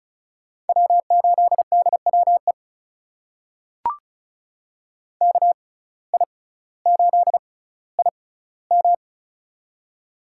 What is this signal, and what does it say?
Morse code, heard as a keyed tone, about 36 wpm, spaced as for 18 wpm, 700 Hz. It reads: W8DWE KI8IM
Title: USA Call Signs - 5 Characters - 6x Character Spacing - 35wpm